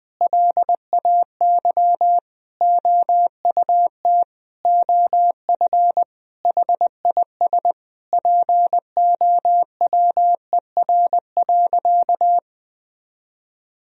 Morse code, heard as a keyed tone, about 20 wpm, 700 Hz. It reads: LAY OUT OF HIS POWER.
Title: Sentences from Top 1000 Words - Review Part 2 - 20wpm